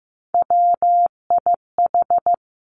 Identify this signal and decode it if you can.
Morse code, heard as a keyed tone, about 15 wpm, 700 Hz. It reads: WIH
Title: Three Challenging Letter-Number Combinations 15wpm